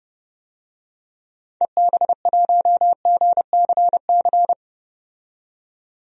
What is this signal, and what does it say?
Morse code, heard as a keyed tone, about 30 wpm, 700 Hz. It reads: EB1GCC
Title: DXCC-100 Call Signs - 6 Characters - 30wpm